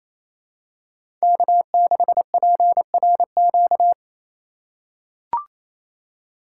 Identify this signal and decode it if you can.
Morse code, heard as a keyed tone, about 28 wpm, 700 Hz. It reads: K6PRQ
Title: USA Call Signs - 5 Characters - 4x Character Spacing - 28wpm